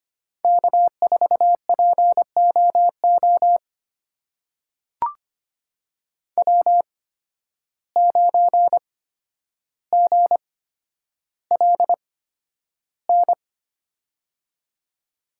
Morse code, heard as a keyed tone, about 25 wpm, spaced as for 11 wpm, 700 Hz. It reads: K4POO W9GLN
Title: USA Call Signs - 5 Characters - 8x Character Spacing - 25wpm